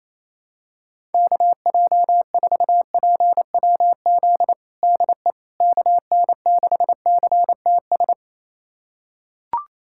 Morse code, heard as a keyed tone, about 28 wpm, 700 Hz. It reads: KJ4PWZ DE KN6CTS
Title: USA Callsign DE Callsign 28wpm